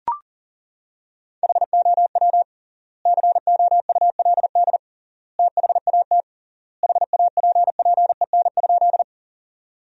Morse code, heard as a keyed tone, about 37 wpm, 700 Hz. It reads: HOW COULD THAT HAPPEN?
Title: 3x Word-Spacing - Sentences from Top 400 Words - 40wpm